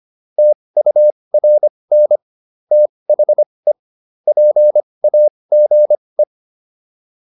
Morse code, heard as a keyed tone, about 25 wpm, spaced as for 20 wpm, 600 Hz. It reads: TURN THE PAGE